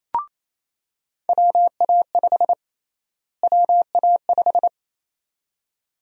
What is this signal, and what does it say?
Morse code, heard as a keyed tone, about 28 wpm, 700 Hz. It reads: WA5 WA5
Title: USA Call Sign Prefixes - Repeated 2x - 28wpm